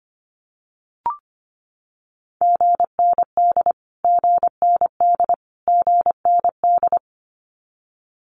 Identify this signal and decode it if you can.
Morse code, heard as a keyed tone, about 25 wpm, 700 Hz. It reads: GND GND GND
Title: Abbreviations Repeated 3x - 25wpm